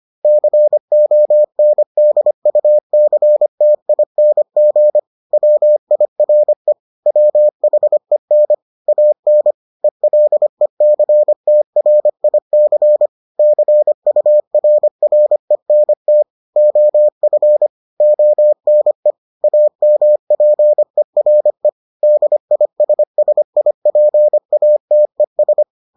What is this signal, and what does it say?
Morse code, heard as a keyed tone, about 25 wpm, 600 Hz. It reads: CONDUCTING WIRE WHEN AN ELECTRIC CURRENT OF ONE AMPERE DISSIPATES